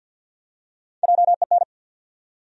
Morse code, heard as a keyed tone, about 53 wpm, 700 Hz. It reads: JEN
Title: USA Common Names - Max 3 Letters - 50wpm